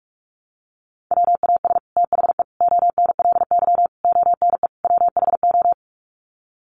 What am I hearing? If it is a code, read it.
Morse code, heard as a keyed tone, about 44 wpm, 700 Hz. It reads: WAS THE ONLY ONE WHO
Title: Sets of 5 Words Encore2 45wpm